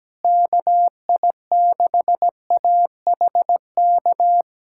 Morse code, heard as a keyed tone, about 17 wpm, 700 Hz. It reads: KI6AHK
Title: USA Callsign DE Callsign - 6 Characters - 17wpm